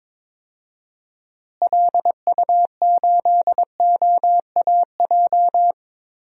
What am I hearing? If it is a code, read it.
Morse code, heard as a keyed tone, about 22 wpm, 700 Hz. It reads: LU8OAJ
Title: DXCC-100 Call Signs - 6 Characters - Repeated 2x - 22wpm